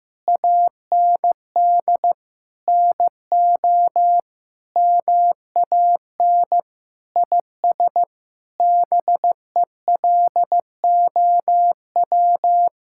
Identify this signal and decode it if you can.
Morse code, heard as a keyed tone, about 16 wpm, 700 Hz. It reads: AND NO MAN IS BELOW
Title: Sentences from Top 1000 Words - Review Part 5 - 15wpm